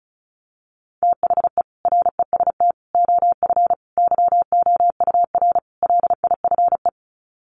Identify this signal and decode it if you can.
Morse code, heard as a keyed tone, about 34 wpm, 700 Hz. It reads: THE REST OF YOUR LIFE
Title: Sets of 5 Words 35wpm